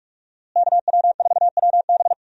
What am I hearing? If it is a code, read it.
Morse code, heard as a keyed tone, about 46 wpm, 700 Hz. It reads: KW4WB
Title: USA Call Signs - 5 Characters - Repeated 2x - 45wpm